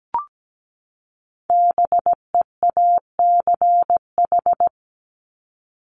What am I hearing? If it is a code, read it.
Morse code, heard as a keyed tone, about 17 wpm, 700 Hz. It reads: BEACH
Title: Top 1251-1500 Words 17wpm